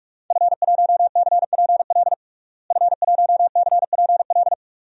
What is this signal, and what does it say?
Morse code, heard as a keyed tone, about 42 wpm, 700 Hz. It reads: F1CPL F1CPL
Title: DXCC-100 Call Signs - 5 Characters - Repeated 2x - 45wpm